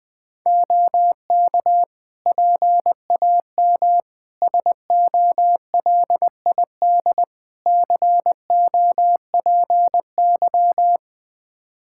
Morse code, heard as a keyed tone, about 21 wpm, 700 Hz. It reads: OK PAM SOLID COPY
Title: Practice QSO - Exchange 3 - More Info - 20wpm